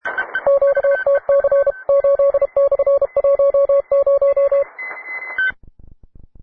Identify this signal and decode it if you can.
Morse code, heard as a keyed tone, about 32 wpm, 560 Hz. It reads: QTC 8/10